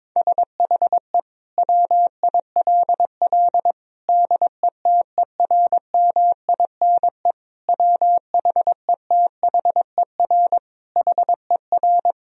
Morse code, heard as a keyed tone, about 22 wpm, 700 Hz. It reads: SHE WILL DETERMINE WHETHER HER
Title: Sentences from Top 1000 Words - Review Part 4 - 22wpm